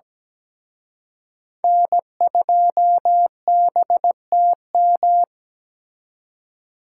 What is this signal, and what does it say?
Morse code, heard as a keyed tone, about 20 wpm, 700 Hz. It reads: N2BTM
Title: Speed Racing - USA Call Signs 17wpm